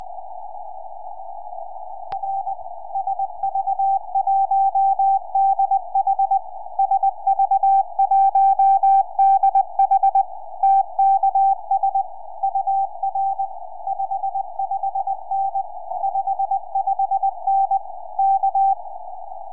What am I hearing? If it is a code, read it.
Morse code, heard as a keyed tone, about 20 wpm, 780 Hz. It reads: R SV1DH SV1DH TKS UR 55N 55N K
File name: oy3je calling sv1dh 4m